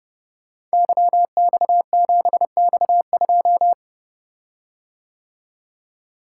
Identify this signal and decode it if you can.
Morse code, heard as a keyed tone, about 30 wpm, 700 Hz. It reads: YX7X2
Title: Code Groups - 5 Letter:Numbers - 30wpm